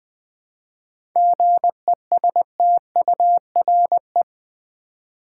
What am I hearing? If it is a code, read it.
Morse code, heard as a keyed tone, about 20 wpm, 700 Hz. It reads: GESTURE
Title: 7 Letter Words - First 3 Letters Given - 20wpm